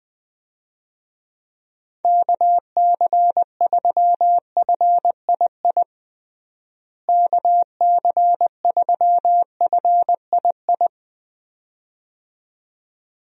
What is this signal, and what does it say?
Morse code, heard as a keyed tone, about 21 wpm, 700 Hz. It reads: KC3FII KC3FII
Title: USA Call Signs - 6 Characters - Repeated 2x - 20wpm